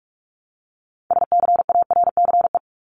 Morse code, heard as a keyed tone, about 44 wpm, 700 Hz. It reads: SCARCE
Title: LY Suffix - Top 100 Words with Base Words 45wpm